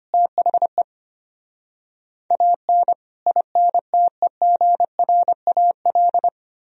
Morse code, heard as a keyed tone, about 33 wpm, 700 Hz. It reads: THE AN INTEGRAL